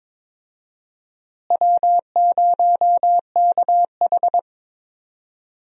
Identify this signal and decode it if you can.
Morse code, heard as a keyed tone, about 22 wpm, 700 Hz. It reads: W0KH
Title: USA Call Signs - 4 Characters - Repeated 2x - 22wpm